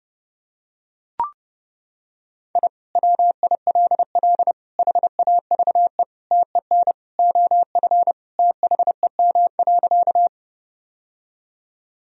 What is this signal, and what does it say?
Morse code, heard as a keyed tone, about 30 wpm, 700 Hz. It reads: I WILL HAVE TEN OF THEM.
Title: Sentences from Top 1000 Words - Review Part 2 - 30wpm